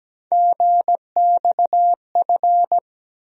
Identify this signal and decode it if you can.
Morse code, heard as a keyed tone, about 17 wpm, 700 Hz. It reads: GXF
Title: USA Call Sign Postfixes - 3 Letters 17wpm